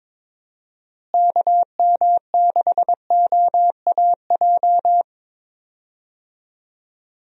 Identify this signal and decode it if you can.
Morse code, heard as a keyed tone, about 22 wpm, 700 Hz. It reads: KM6OAJ
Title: USA Call Signs - 6 Characters - Repeated 3x - 22wpm